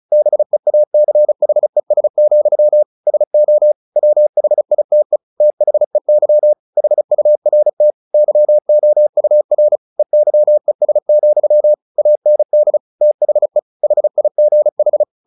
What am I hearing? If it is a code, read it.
Morse code, heard as a keyed tone, about 34 wpm, 600 Hz. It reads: BEACHES, SO WHITE THEY HURT YOUR EYES, AND THE HIGH
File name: the old man and the sea / the old man and the sea 1 35wpm